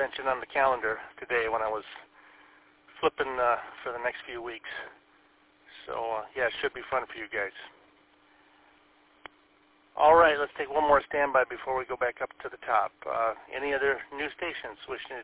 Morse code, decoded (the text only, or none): none